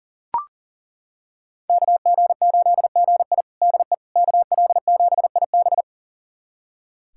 Morse code, heard as a keyed tone, about 40 wpm, 700 Hz. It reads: KG8GI DE KL7IB
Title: USA Callsign DE Callsign 40wpm